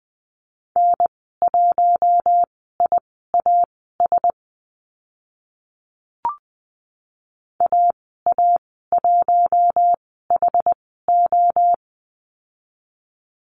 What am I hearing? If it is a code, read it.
Morse code, heard as a keyed tone, about 20 wpm, spaced as for 14 wpm, 700 Hz. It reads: N1IAS AA1HO
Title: 062 Lesson 11g - Letter h - Call Signs 20wpm